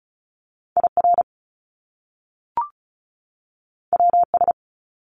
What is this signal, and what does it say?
Morse code, heard as a keyed tone, about 36 wpm, 700 Hz. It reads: IR WS
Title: Top Two Letter Combinations in Words 35wpm